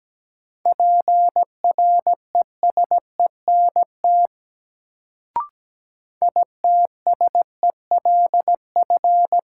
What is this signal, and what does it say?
Morse code, heard as a keyed tone, about 17 wpm, 700 Hz. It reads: PRESENT ITSELF
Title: Sentences from Top 800 Words - New Word Drill - 17wpm